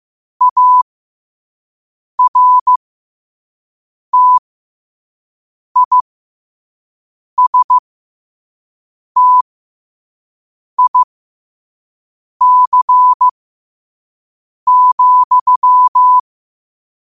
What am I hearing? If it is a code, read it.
Morse code, heard as a keyed tone, about 15 wpm, spaced as for 5 wpm, 1000 Hz. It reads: ARTISTIC,